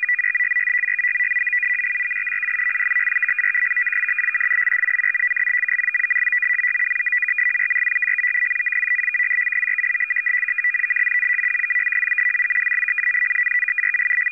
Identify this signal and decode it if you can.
Two-tone teleprinter signal, about 450 Hz shift.
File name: POL- -0900and0905utc-fsk100-500